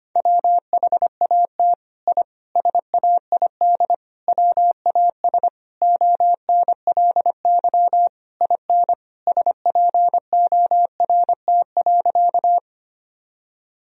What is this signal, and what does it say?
Morse code, heard as a keyed tone, about 25 wpm, 700 Hz. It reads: WHAT I SAID WAS ONLY IN SPORT.
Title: Sentences from Top 1000 Words - Review Part 4 - 25wpm